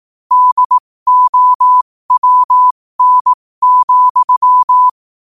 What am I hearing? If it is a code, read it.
Morse code, heard as a keyed tone, about 18 wpm, spaced as for 15 wpm, 1000 Hz. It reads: DOWN,
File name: qotdfiles / 15wpm